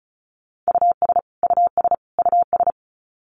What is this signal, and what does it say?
Morse code, heard as a keyed tone, about 34 wpm, 700 Hz. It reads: US US US
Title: Top 500 Words Repeated 35wpm